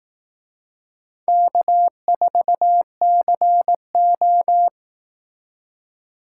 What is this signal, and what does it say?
Morse code, heard as a keyed tone, about 18 wpm, 700 Hz. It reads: K4CO
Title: Speed Racing - USA Call Signs 15wpm